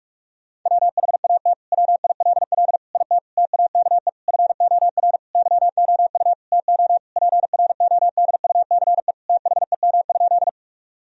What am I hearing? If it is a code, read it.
Morse code, heard as a keyed tone, about 42 wpm, 700 Hz. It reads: WHAT WILL IT TAKE FOR YOU TO PRODUCE THEM?
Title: Sentences from Top 1000 Words - Review Part 3 - 45wpm